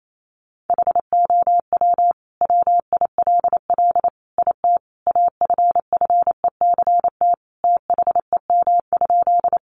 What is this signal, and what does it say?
Morse code, heard as a keyed tone, about 28 wpm, 700 Hz. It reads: HOW WILL IT AFFECT THEM?
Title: Sentences from Top 900 Words 28wpm